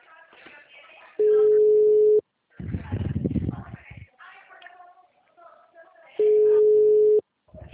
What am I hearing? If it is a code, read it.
Ringback tone (Europe, 425 Hz), with speech.
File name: out- -2010- - - .88616